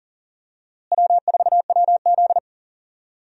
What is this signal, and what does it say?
Morse code, heard as a keyed tone, about 38 wpm, 700 Hz. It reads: W4WZ
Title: USA Call Signs - 4 Characters - Repeated 3x - 40wpm